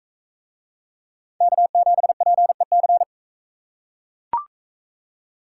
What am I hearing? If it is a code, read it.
Morse code, heard as a keyed tone, about 42 wpm, 700 Hz. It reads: K7PEC